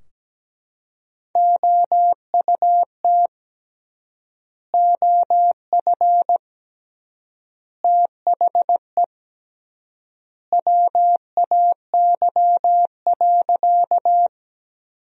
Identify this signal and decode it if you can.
Morse code, heard as a keyed tone, about 17 wpm, 700 Hz. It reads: OUT OF THE WAY.
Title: 3x Word-Spacing - Sentences from Top 100 Words - 17wpm